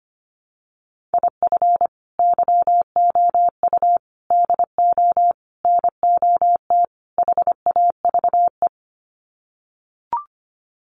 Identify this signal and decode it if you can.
Morse code, heard as a keyed tone, about 25 wpm, 700 Hz. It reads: IF YOU DO NOT HAVE
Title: Sets of 5 Words 25wpm